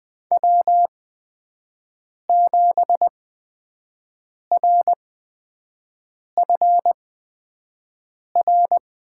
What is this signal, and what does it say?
Morse code, heard as a keyed tone, about 20 wpm, spaced as for 5 wpm, 700 Hz. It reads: W7RFR
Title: USA Call Signs - 8x Character Spacing - 20wpm